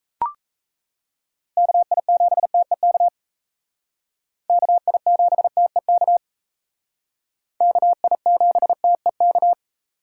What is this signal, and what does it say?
Morse code, heard as a keyed tone, about 36 wpm, 700 Hz. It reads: KI7TEK KI7TEK KI7TEK